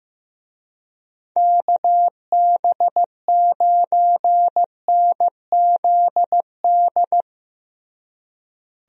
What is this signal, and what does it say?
Morse code, heard as a keyed tone, about 15 wpm, 700 Hz. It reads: KB9NZD
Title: USA Call Signs - 6 Characters - Repeated 2x - 15wpm